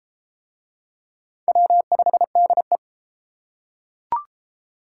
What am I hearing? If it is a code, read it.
Morse code, heard as a keyed tone, about 31 wpm, 700 Hz. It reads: W5DE